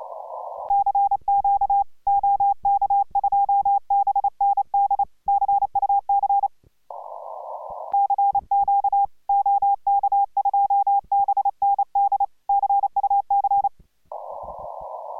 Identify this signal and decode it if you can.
Morse code, heard as a keyed tone, about 29 wpm, 800 Hz. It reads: CQ OK2BND CUC CQ OK2BND CUC